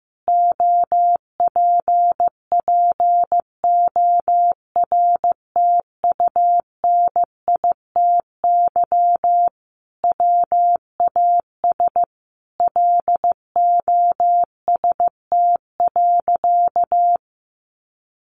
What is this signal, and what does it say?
Morse code, heard as a keyed tone, about 15 wpm, 700 Hz. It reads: OPPORTUNITY WAS LOST.
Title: Sentences from Top 1400 Words 15wpm